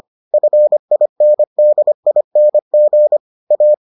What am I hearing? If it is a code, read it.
Morse code, heard as a keyed tone, about 25 wpm, 600 Hz. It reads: FINDING A